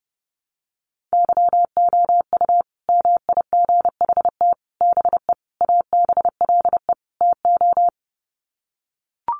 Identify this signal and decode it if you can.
Morse code, heard as a keyed tone, about 30 wpm, 700 Hz. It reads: YOU MIGHT BE ABLE TO
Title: Sets of 5 Words Encore2 30wpm